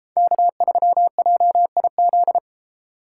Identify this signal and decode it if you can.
Morse code, heard as a keyed tone, about 33 wpm, 700 Hz. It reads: K3JIZ